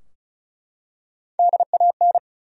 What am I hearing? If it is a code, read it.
Morse code, heard as a keyed tone, about 35 wpm, 700 Hz. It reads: DAN
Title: USA Common Names - Max 5 Letters - 35wpm